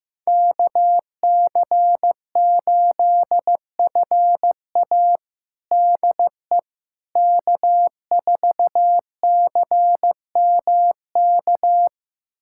Morse code, heard as a keyed tone, about 15 wpm, 700 Hz. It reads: KC8FA DE K4CMK
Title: USA Callsign DE Callsign - 5 Characters - 15wpm